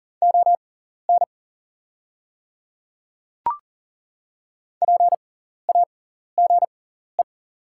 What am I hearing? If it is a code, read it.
Morse code, heard as a keyed tone, about 40 wpm, spaced as for 14 wpm, 700 Hz. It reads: ON PAGE
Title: 6x Character Spacing - Top 300 Words - 40wpm